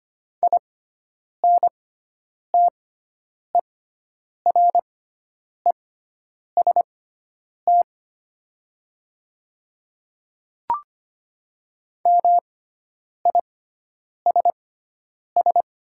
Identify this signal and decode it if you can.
Morse code, heard as a keyed tone, about 25 wpm, spaced as for 9 wpm, 700 Hz. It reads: INTEREST MISS
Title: 6x Character Spacing - Top 500 Words - 25wpm